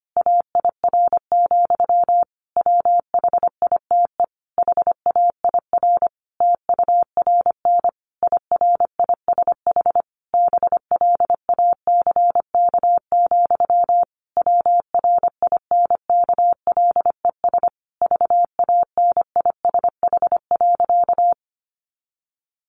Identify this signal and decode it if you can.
Morse code, heard as a keyed tone, about 25 wpm, 700 Hz. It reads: AIR, WHITE HAIR TURN IRISH BLACK, WRINKLES VANISH.
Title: A Sound Of Thunder 25wpm